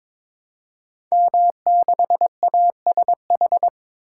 Morse code, heard as a keyed tone, about 22 wpm, 700 Hz. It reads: M6ASH
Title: Code Groups - 5 Letter:Numbers - 22wpm